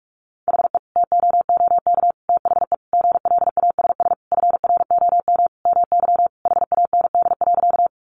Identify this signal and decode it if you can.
Morse code, heard as a keyed tone, about 44 wpm, 700 Hz. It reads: HE TOOK THE GLASS FROM MY HAND.
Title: Sentences from Top 1300 Words 45wpm